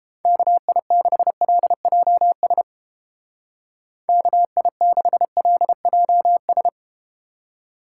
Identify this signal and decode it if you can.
Morse code, heard as a keyed tone, about 30 wpm, 700 Hz. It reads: KI6LJS KI6LJS